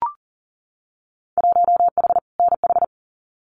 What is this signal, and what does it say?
Morse code, heard as a keyed tone, about 40 wpm, 700 Hz. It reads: 1H NH
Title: Winter Field Day Exchanges 40wpm